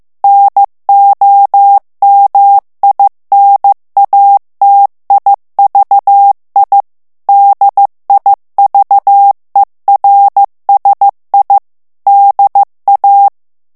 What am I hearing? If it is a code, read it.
Morse code, heard as a keyed tone, about 15 wpm, 800 Hz. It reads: NOMINATIVI DIVERSI DA
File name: morse mp3 / ITALIEN texte clair 2 vitesse 14 mots minute